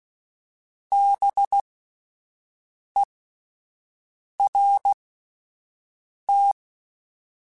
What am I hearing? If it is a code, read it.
Morse code, heard as a keyed tone, about 16 wpm, spaced as for 5 wpm, 800 Hz. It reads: BERT